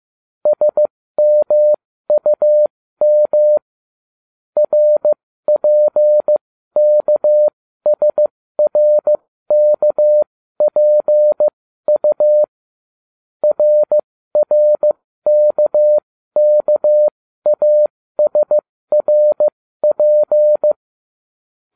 Morse code, heard as a keyed tone, about 15 wpm, spaced as for 13 wpm, 600 Hz. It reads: SMUM RPKSRKPU RRKKASRP